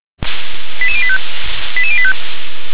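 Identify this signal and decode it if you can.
Five-tone selective call (ZVEI), heard as quick runs of five tones, tones of about 70 ms. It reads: 88085 88085
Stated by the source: ZVEI five-tone call received on a software-defined radio.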